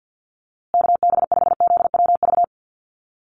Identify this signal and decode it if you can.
Morse code, heard as a keyed tone, about 50 wpm, 700 Hz. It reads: KB5ZWV
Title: USA Call Signs Repeated 3x 50wpm